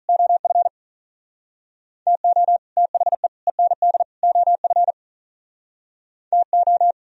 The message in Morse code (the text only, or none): OF TO THE END OF TO